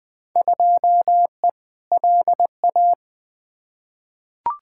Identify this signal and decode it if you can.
Morse code, heard as a keyed tone, about 20 wpm, 700 Hz. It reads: 2E LA
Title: Field Day Exchanges Encore - 20wpm